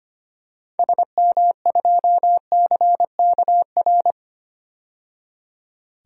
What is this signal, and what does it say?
Morse code, heard as a keyed tone, about 25 wpm, 700 Hz. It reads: SM2CKR
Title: DXCC-100 Call Signs - 6 Characters - 25wpm